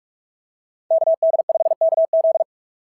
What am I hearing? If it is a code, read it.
Morse code, heard as a keyed tone, about 46 wpm, 690 Hz. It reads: KD5KZ
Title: SOTA QSO Practice 45wpm